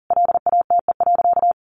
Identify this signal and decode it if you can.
Morse code, heard as a keyed tone, about 40 wpm, 700 Hz. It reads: LATE.
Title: Sentences from Top 300 Words 40wpm